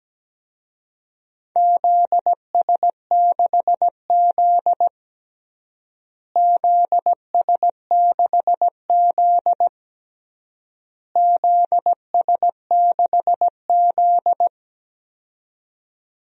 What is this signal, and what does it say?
Morse code, heard as a keyed tone, about 17 wpm, 700 Hz. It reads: ZS6Z ZS6Z ZS6Z
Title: DXCC-100 Call Signs - 4 Characters - Repeated 3x - 17wpm